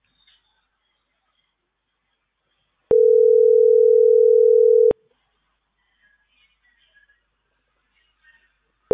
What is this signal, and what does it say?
Ringback tone (North America), 440 + 480 Hz.